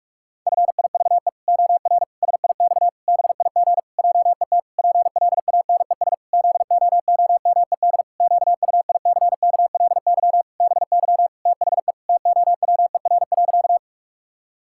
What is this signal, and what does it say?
Morse code, heard as a keyed tone, about 41 wpm, 700 Hz. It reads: FIVE OR SIX BIG JET PLANES ZOOMED QUICKLY BY THE TOWER.
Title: Pangram Challenge v2 - 45wpm